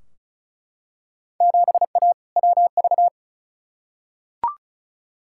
Morse code, heard as a keyed tone, about 36 wpm, 700 Hz. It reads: 7A WV
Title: Field Day Exchanges Encore - 35wpm